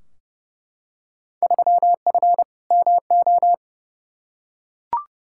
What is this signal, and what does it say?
Morse code, heard as a keyed tone, about 30 wpm, 700 Hz. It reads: 3F MO
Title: Field Day Exchanges Encore - 30wpm